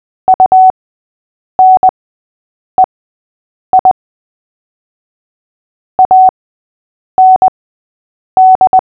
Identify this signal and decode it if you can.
Morse code, heard as a keyed tone, about 20 wpm, spaced as for 8 wpm, 750 Hz. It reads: UNEI AND
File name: cw / fists / Short-Story-3-8wpm